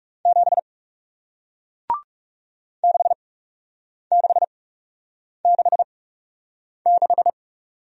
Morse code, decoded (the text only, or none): 7 6666